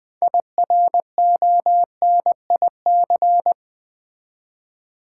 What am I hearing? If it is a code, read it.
Morse code, heard as a keyed tone, about 20 wpm, 700 Hz. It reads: IRONIC
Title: Build A Word - 2 Letters At A Time - 6-Letter Words - 20wpm